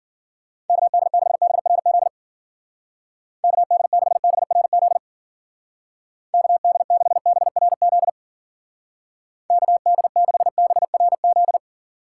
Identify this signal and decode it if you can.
Morse code, heard as a keyed tone, about 50 wpm, 700 Hz. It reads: KD6BRZ KD6BRZ KD6BRZ KD6BRZ